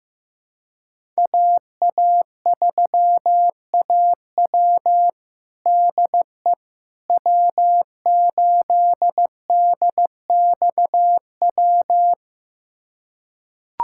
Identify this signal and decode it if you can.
Morse code, heard as a keyed tone, about 16 wpm, 700 Hz. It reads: AA3AW DE W8DXW
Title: USA Callsign DE Callsign - 5 Characters - 15wpm